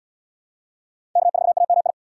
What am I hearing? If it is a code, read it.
Morse code, heard as a keyed tone, about 77 wpm, 700 Hz. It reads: K4AMI